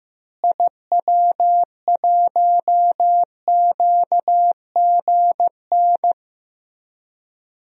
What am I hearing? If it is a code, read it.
Morse code, heard as a keyed tone, about 15 wpm, 700 Hz. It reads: IW1QGN
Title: DXCC-100 Call Signs - 6 Characters - Repeated 2x - 15wpm